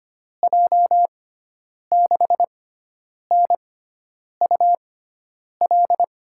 Morse code, heard as a keyed tone, about 25 wpm, spaced as for 9 wpm, 700 Hz. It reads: J6NUL
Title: USA Call Signs - 6 Characters - 6x Character Spacing - 25wpm